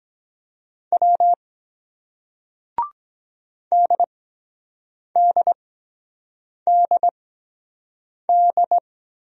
Morse code, heard as a keyed tone, about 23 wpm, spaced as for 7 wpm, 700 Hz. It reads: W DDDD